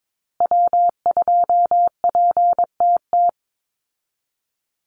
Morse code, heard as a keyed tone, about 22 wpm, 700 Hz. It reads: W2PTT
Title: Call Signs Repeated 3x - 22wpm